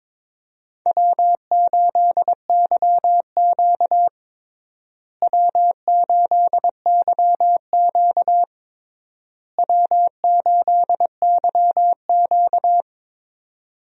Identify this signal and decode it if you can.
Morse code, heard as a keyed tone, about 22 wpm, 700 Hz. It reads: W8YQ W8YQ W8YQ
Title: USA Call Signs - 4 Characters - Repeated 3x - 22wpm